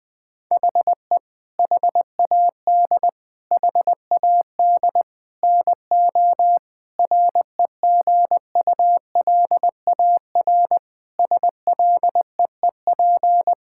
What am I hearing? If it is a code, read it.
Morse code, heard as a keyed tone, about 20 wpm, 700 Hz. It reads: HE HAD HAD NO REGULAR SLEEP
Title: Sentences from Top 1000 Words - Review Part 6 - 20wpm